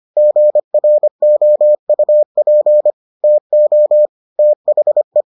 Morse code, heard as a keyed tone, about 25 wpm, 600 Hz. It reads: GROUP TO THE